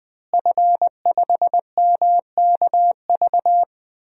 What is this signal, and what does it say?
Morse code, heard as a keyed tone, about 21 wpm, 700 Hz. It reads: F5MKV
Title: DXCC-100 Call Signs - 5 Characters - Repeated 3x - 20wpm